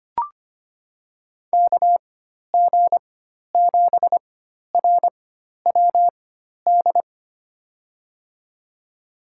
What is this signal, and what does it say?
Morse code, heard as a keyed tone, about 25 wpm, spaced as for 12 wpm, 700 Hz. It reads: KG7RWD
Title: USA Call Signs - 6 Characters - 4x Character Spacing - 25wpm